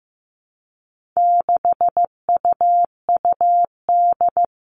Morse code, heard as a keyed tone, about 15 wpm, 700 Hz. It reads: 6UUD
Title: All Cut Numbers - 4-Digit 15wpm